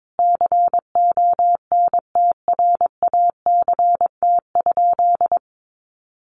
Morse code, heard as a keyed tone, about 22 wpm, 700 Hz. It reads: CONTRACT?
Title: Sentences from Top 1400 Words 22wpm